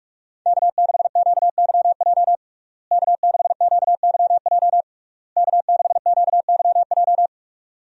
Morse code, heard as a keyed tone, about 41 wpm, 700 Hz. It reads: K6QYJ K6QYJ K6QYJ
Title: USA Call Signs - 5 Characters - Repeated 3x - 45wpm